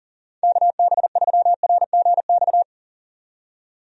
Morse code, heard as a keyed tone, about 40 wpm, 700 Hz. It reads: KB3RGX
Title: USA Call Signs - 6 Characters - 40wpm